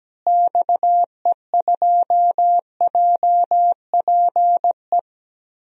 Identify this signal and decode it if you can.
Morse code, heard as a keyed tone, about 17 wpm, 700 Hz. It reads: XE2JPE
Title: DXCC-100 Call Signs - 6 Characters - Repeated 2x - 17wpm